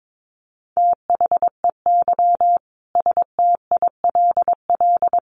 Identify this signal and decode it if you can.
Morse code, heard as a keyed tone, about 22 wpm, 700 Hz. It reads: THEY STILL
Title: Sentences from Top 1200 Words 22wpm